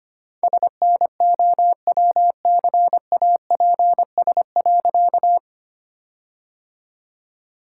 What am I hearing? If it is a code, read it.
Morse code, heard as a keyed tone, about 25 wpm, 700 Hz. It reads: SNOWCAPS.
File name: practice / 25wpm / All-Age Appropriate Jokes - 25wpm